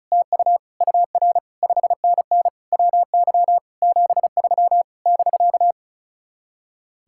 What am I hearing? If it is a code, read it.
Morse code, heard as a keyed tone, about 44 wpm, 700 Hz. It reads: TU UR 5NN WY 73 <BK>